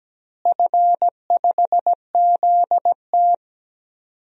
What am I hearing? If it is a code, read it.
Morse code, heard as a keyed tone, about 17 wpm, 700 Hz. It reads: F5ZT
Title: DXCC-100 Call Signs - 4 Characters - Repeated 3x - 17wpm